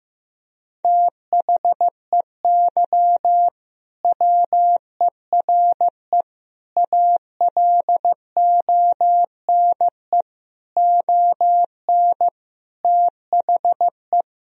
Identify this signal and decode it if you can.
Morse code, heard as a keyed tone, about 16 wpm, 700 Hz. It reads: THEY WERE ALONE ON THE
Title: Sentences from Top 1000 Words - Review Part 3 - 15wpm